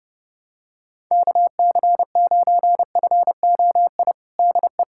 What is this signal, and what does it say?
Morse code, heard as a keyed tone, about 30 wpm, 700 Hz. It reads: KC9FOI DE
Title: USA Callsign DE Callsign - 6 Characters - 30wpm